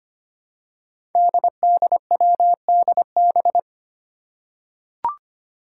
Morse code, heard as a keyed tone, about 25 wpm, 700 Hz. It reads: DDWDB
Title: Five Challenging Letter-Number Combinations 25wpm